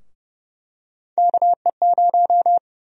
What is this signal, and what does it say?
Morse code, heard as a keyed tone, about 30 wpm, 700 Hz. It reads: KE0
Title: USA Call Sign Prefixes - 30wpm